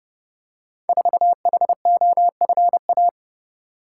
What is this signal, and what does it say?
Morse code, heard as a keyed tone, about 28 wpm, 700 Hz. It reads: 4HOFA
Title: Code Groups - 5 Letter:Numbers - 30wpm